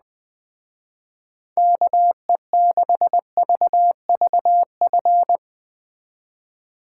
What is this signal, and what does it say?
Morse code, heard as a keyed tone, about 20 wpm, 700 Hz. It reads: KE6VVF
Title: Speed Racing - USA 2x3 Call Signs 17wpm